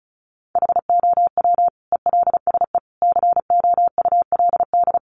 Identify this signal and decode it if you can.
Morse code, heard as a keyed tone, about 34 wpm, 700 Hz. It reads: HOW ELSE COULD
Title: Sentences from Top 1300 Words 35wpm